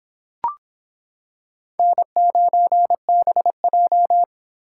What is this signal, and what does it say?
Morse code, heard as a keyed tone, about 25 wpm, 700 Hz. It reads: N9BJ